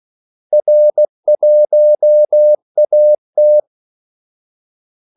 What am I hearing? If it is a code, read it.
Morse code, heard as a keyed tone, about 16 wpm, 600 Hz. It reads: R1AT
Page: Kaldesignaler 16 wpm